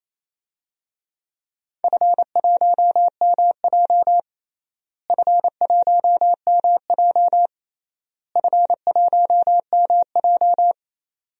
Morse code, heard as a keyed tone, about 28 wpm, 700 Hz. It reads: F1MJ F1MJ F1MJ
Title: DXCC-100 Call Signs - 4 Characters - Repeated 3x - 28wpm